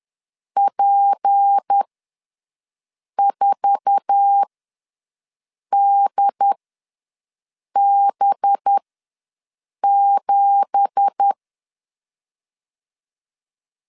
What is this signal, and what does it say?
Morse code, heard as a keyed tone, about 10 wpm, spaced as for 5 wpm, 800 Hz. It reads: P4DB7
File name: LEZIONI / 10 SIGNS / 10A SLASH ws 10wpm 4121